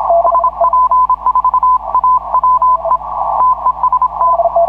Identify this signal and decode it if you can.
Morse code, heard as a keyed tone, about 26 wpm, 1010 Hz. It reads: SP4AWE TEST